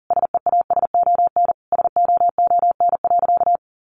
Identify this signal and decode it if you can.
Morse code, heard as a keyed tone, about 37 wpm, 700 Hz. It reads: SEASON SOON.
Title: Sentences from Top 1400 Words 40wpm